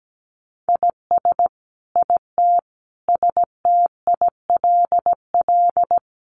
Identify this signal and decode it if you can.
Morse code, heard as a keyed tone, about 17 wpm, 700 Hz. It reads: IS IT STILL
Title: Sentences from Top 1500 Words 17wpm